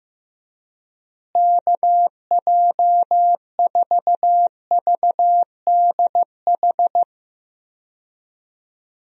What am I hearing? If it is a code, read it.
Morse code, heard as a keyed tone, about 15 wpm, 700 Hz. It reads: KJ4VDH
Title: USA Call Signs - 6 Characters - Repeated 2x - 15wpm